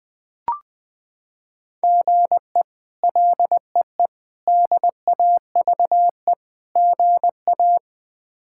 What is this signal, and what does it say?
Morse code, heard as a keyed tone, about 20 wpm, 700 Hz. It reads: GE LEE DAVE GA
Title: K1USN SST QSO Practice - 20wpm